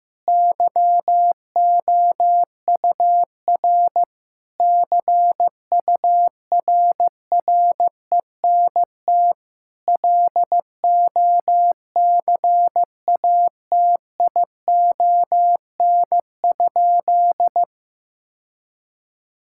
Morse code, heard as a keyed tone, about 15 wpm, 700 Hz. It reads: YOUR CURRENT LOCATION?
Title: Sentences from Top 1000 Words - Review Part 6 - 15wpm